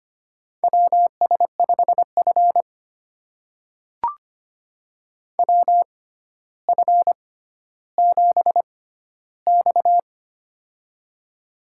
Morse code, heard as a keyed tone, about 25 wpm, spaced as for 13 wpm, 700 Hz. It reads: WS5F WF7X
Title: USA Call Signs - 4 Characters - 6x Character Spacing - 25wpm